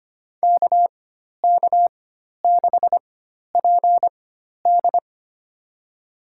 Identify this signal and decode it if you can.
Morse code, heard as a keyed tone, about 25 wpm, spaced as for 12 wpm, 700 Hz. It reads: KK6PD
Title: USA Call Signs - 5 Characters - 4x Character Spacing - 25wpm